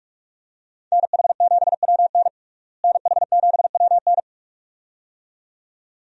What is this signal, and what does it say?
Morse code, heard as a keyed tone, about 44 wpm, 700 Hz. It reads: NH7WN NH7WN
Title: USA Call Signs - 5 Characters - Repeated 2x - 45wpm